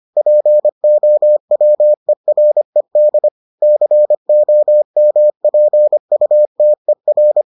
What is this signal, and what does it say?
Morse code, heard as a keyed tone, about 25 wpm, 600 Hz. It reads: POWERED COMPUTER